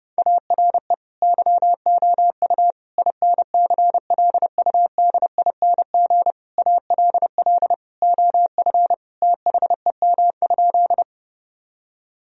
Morse code, heard as a keyed tone, about 31 wpm, 700 Hz. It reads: ARE YOU INCLUDING ALL OF THEM?
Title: Sentences from Top 1000 Words - Review Part 4 - 30wpm